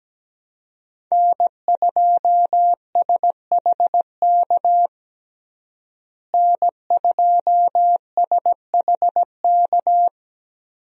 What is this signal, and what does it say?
Morse code, heard as a keyed tone, about 17 wpm, 700 Hz. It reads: N2SHK N2SHK
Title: USA Call Signs - 5 Characters - Repeated 3x - 17wpm